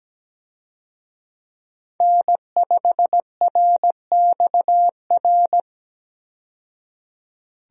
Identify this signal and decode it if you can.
Morse code, heard as a keyed tone, about 17 wpm, 700 Hz. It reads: N5RXR
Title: USA Call Signs - 5 Characters - 17wpm